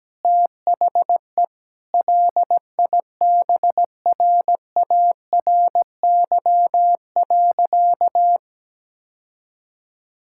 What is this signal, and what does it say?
Morse code, heard as a keyed tone, about 17 wpm, 700 Hz. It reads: THE LIBRARY.
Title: Sentences from Top 1000 Words - Review Part 6 - 17wpm